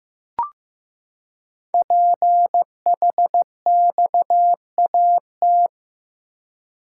Morse code, heard as a keyed tone, about 15 wpm, 700 Hz. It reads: PHXAT
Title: Code Groups - 5 Letter:Numbers - 15wpm